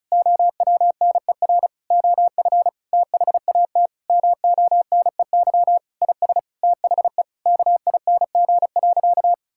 Morse code, heard as a keyed tone, about 36 wpm, 700 Hz. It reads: OWNER OF THAT MONEY IS THE KING.
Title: Sentences from Top 1600 Words 35wpm